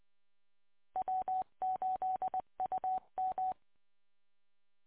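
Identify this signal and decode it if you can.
Morse code, heard as a keyed tone, about 23 wpm, 750 Hz. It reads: W8UM
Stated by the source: Morse code from W8UM on 145.230.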